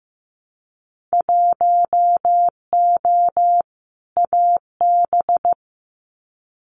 Morse code, heard as a keyed tone, about 15 wpm, 700 Hz. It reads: 1O AB
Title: Winter Field Day Exchanges Encore 15wpm